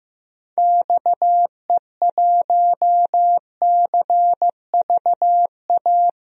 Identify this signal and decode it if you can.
Morse code, heard as a keyed tone, about 15 wpm, 700 Hz. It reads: XE1CVA
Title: DXCC-100 Call Signs - 6 Characters - 15wpm